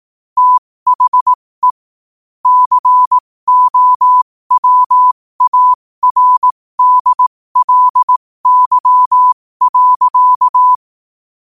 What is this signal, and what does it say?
Morse code, heard as a keyed tone, about 18 wpm, spaced as for 15 wpm, 1000 Hz. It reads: THE COWARDLY.
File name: qotdfiles / 15wpm